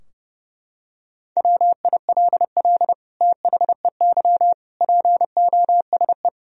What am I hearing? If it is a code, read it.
Morse code, heard as a keyed tone, about 32 wpm, 700 Hz. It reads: WILL THEY POSE
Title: Sentences from Top 1000 Words - Review Part 2 - 30wpm